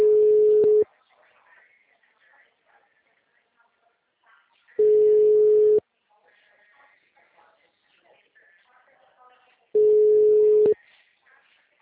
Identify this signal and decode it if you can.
Ringback tone (Europe, 425 Hz), with speech.